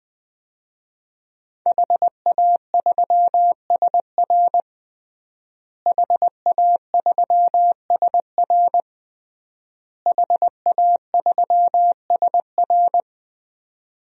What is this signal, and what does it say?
Morse code, heard as a keyed tone, about 20 wpm, 700 Hz. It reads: HA3SR HA3SR HA3SR
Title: DXCC-100 Call Signs - 5 Characters - Repeated 3x - 20wpm